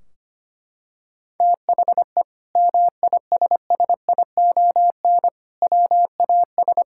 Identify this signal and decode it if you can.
Morse code, heard as a keyed tone, about 25 wpm, 700 Hz. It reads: THE MISSION WAS
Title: Sentences from Top 1600 Words 25wpm